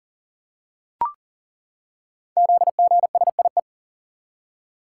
Morse code, heard as a keyed tone, about 43 wpm, 690 Hz. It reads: ZGSIE
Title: Code Groups - 5 Letters - 40wpm